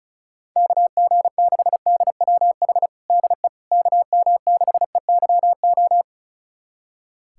Morse code, heard as a keyed tone, about 36 wpm, 700 Hz. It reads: KG6DWH DE KM6EYO
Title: USA Callsign DE Callsign - 6 Characters - 35wpm